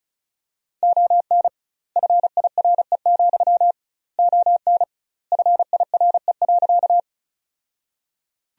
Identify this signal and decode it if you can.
Morse code, heard as a keyed tone, about 34 wpm, 700 Hz. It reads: ON FIRE, ON FIRE.
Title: 2x Word-Spacing - Sentences from Top 400 Words - 35wpm